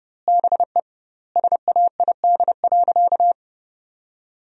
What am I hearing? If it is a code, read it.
Morse code, heard as a keyed tone, about 32 wpm, 700 Hz. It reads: BE SAID.
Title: 2x Word-Spacing - Sentences from Top 500 Words - 30wpm